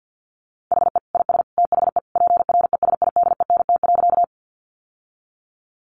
Morse code, heard as a keyed tone, about 50 wpm, 700 Hz. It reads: HE IS THE PRESIDENT.
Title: Sentences from Top 1000 Words 50wpm